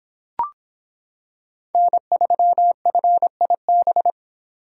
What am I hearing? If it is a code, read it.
Morse code, heard as a keyed tone, about 26 wpm, 700 Hz. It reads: N3FIB